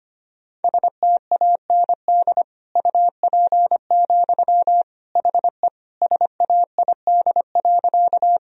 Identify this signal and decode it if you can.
Morse code, heard as a keyed tone, about 25 wpm, 700 Hz. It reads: STAND UP, HE SAID.
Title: Sentences from Top 1000 Words - Review Part 1 - 25wpm